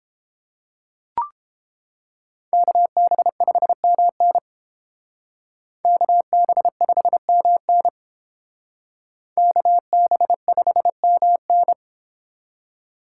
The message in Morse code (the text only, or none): KB5MN KB5MN KB5MN